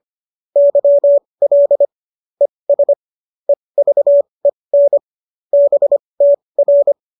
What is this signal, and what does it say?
Morse code, heard as a keyed tone, about 25 wpm, spaced as for 20 wpm, 570 Hz. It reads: YL ES EVEN BTR